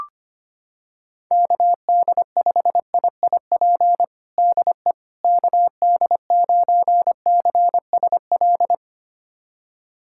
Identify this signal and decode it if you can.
Morse code, heard as a keyed tone, about 25 wpm, 700 Hz. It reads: KD5IIP DE KD9CSL
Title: USA Callsign DE Callsign - 6 Characters - 25wpm